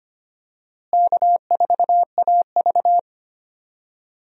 Morse code, heard as a keyed tone, about 25 wpm, 700 Hz. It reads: K4AV
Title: USA Call Signs - 4 Characters - Repeated 3x - 25wpm